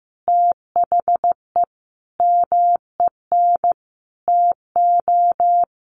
Morse code, heard as a keyed tone, about 15 wpm, 700 Hz. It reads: THE MEN TO
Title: Sentences from Top 1400 Words 15wpm